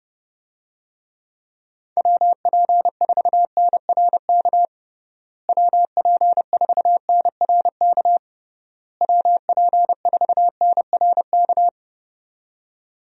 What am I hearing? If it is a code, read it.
Morse code, heard as a keyed tone, about 30 wpm, 700 Hz. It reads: WP4NRK WP4NRK WP4NRK
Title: USA Call Signs - 6 Characters - Repeated 3x - 30wpm